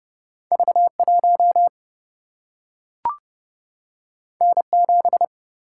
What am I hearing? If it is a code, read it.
Morse code, heard as a keyed tone, about 30 wpm, 700 Hz. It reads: V 1 N 7
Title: Single-Letter Single-Number 30wpm